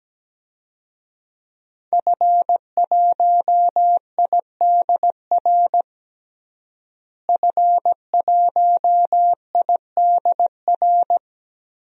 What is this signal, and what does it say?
Morse code, heard as a keyed tone, about 17 wpm, 700 Hz. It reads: F1IDR F1IDR
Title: DXCC-100 Call Signs - 5 Characters - Repeated 3x - 17wpm